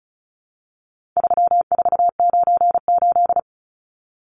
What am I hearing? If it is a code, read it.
Morse code, heard as a keyed tone, about 34 wpm, 700 Hz. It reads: 3498
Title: Numbers - 4-Digit 35wpm